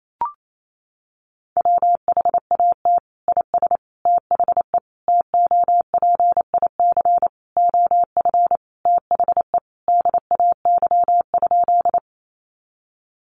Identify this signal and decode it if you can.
Morse code, heard as a keyed tone, about 28 wpm, 700 Hz. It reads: WHAT IS THE TOPIC OF THE DAY?
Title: Sentences from Top 1200 Words 28wpm